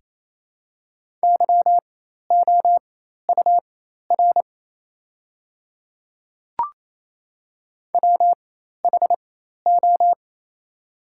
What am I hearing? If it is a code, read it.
Morse code, heard as a keyed tone, about 28 wpm, spaced as for 13 wpm, 700 Hz. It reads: YOUR WHO
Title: 4x Character Spacing - Top 100 Words - 28wpm